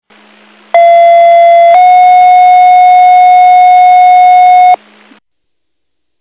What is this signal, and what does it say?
Two-tone page, heard as a one-second tone, then a three-second tone, about 680 then 730 Hz. It is heard from 0.7 to 4.8 s.